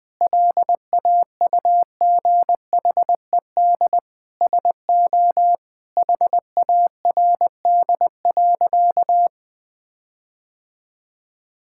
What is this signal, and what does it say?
Morse code, heard as a keyed tone, about 19 wpm, 700 Hz. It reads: LAUGHED SO HARD.
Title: Sentences with 7-Word Phrases - Spoken Beginning - 20wpm